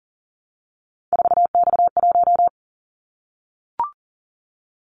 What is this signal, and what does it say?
Morse code, heard as a keyed tone, about 38 wpm, 700 Hz. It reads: 4X1
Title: Callsign Prefixes for Top 50 DXCC - 40wpm